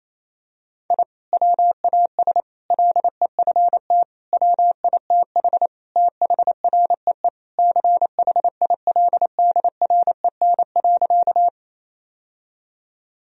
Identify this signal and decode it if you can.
Morse code, heard as a keyed tone, about 28 wpm, 700 Hz. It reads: I WAS LEFT WITH THREE CHILDREN.
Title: Sentences from Top 1000 Words - Review Part 1 - 28wpm